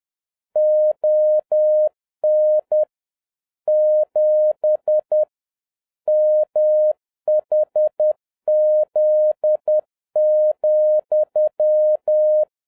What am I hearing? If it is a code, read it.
Morse code, heard as a keyed tone, about 10 wpm, 610 Hz. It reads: ON 7 MHZ,
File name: 2011-07-mp3 / n56 BACK FROM A BIG ACCIDENT 10wpm